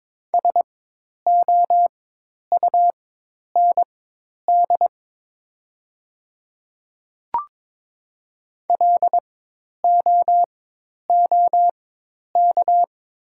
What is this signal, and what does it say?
Morse code, heard as a keyed tone, about 22 wpm, spaced as for 10 wpm, 700 Hz. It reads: SOUND LOOK
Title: 4x Character Spacing - Top 100 Words - 22wpm